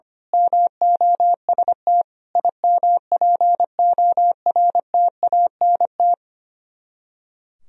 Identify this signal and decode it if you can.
Morse code, heard as a keyed tone, about 24 wpm, 700 Hz. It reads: MOST IMPORTANT
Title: Speed-Racing - Sets of 4 Words 25wpm